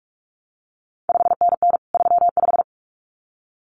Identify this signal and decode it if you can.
Morse code, heard as a keyed tone, about 46 wpm, 700 Hz. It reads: 5NN 35
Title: CQ WW DX Contest Exchange - 45wpm